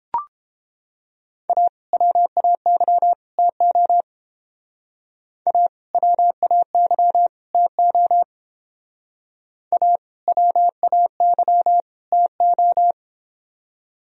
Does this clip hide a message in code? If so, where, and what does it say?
Yes: Morse code from 1.5 to 12.9 s, heard as a keyed tone, about 28 wpm, 700 Hz. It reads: A WAY TO A WAY TO A WAY TO